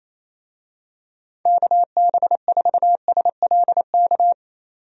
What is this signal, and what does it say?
Morse code, heard as a keyed tone, about 28 wpm, 700 Hz. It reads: KB4SLK
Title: USA Call Signs - 6 Characters - 28wpm